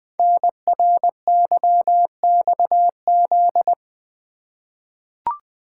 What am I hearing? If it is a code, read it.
Morse code, heard as a keyed tone, about 20 wpm, 700 Hz. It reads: NRYXZ
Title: Code Groups - 5 Letters - 20wpm